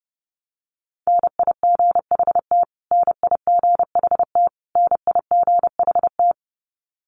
Morse code, heard as a keyed tone, about 30 wpm, 700 Hz. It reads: NIGHT NIGHT NIGHT
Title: Top 500 Words Repeated 30wpm